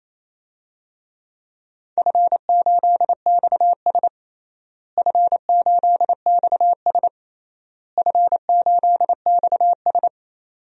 Morse code, heard as a keyed tone, about 28 wpm, 700 Hz. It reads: F8XS F8XS F8XS
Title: DXCC-100 Call Signs - 4 Characters - Repeated 3x - 28wpm